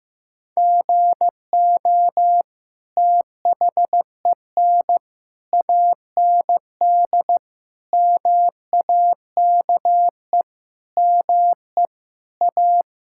Morse code, heard as a keyed tone, about 15 wpm, 700 Hz. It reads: GO THEN AND MAKE ME A
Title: Sentences from Top 1000 Words - Review Part 5 - 15wpm